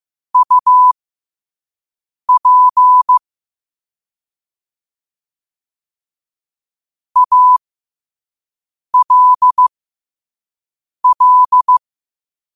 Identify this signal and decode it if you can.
Morse code, heard as a keyed tone, about 15 wpm, spaced as for 5 wpm, 1000 Hz. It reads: UP ALL
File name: qotdfiles / 5wpm